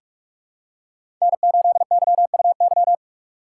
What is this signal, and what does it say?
Morse code, heard as a keyed tone, about 47 wpm, 700 Hz. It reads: N8YUY
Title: USA Call Signs - 5 Characters - 4x Character Spacing - 45wpm